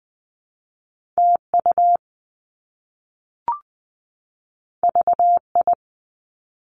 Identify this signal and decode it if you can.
Morse code, heard as a keyed tone, about 20 wpm, 700 Hz. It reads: TU VI
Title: Top Two Letter Combinations in Words 20wpm